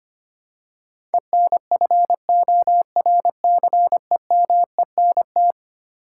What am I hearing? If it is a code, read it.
Morse code, heard as a keyed tone, about 25 wpm, 700 Hz. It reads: ENFORCEMENT
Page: morse-code-ninja-practice-25wpm